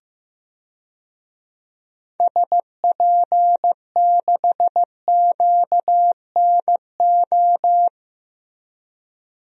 Morse code, heard as a keyed tone, about 16 wpm, 700 Hz. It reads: SP6QNO
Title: DXCC-100 Call Signs - 6 Characters - Repeated 3x - 15wpm